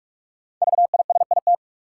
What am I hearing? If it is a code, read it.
Morse code, heard as a keyed tone, about 47 wpm, 700 Hz. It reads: VISIT
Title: Sentences from Top 900 Words - New Word Drill - 45wpm